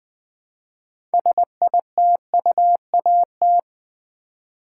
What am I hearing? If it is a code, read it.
Morse code, heard as a keyed tone, about 20 wpm, 700 Hz. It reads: SITUAT
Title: Build A Word - 2 Letters At A Time - 8-Letter Words - 20wpm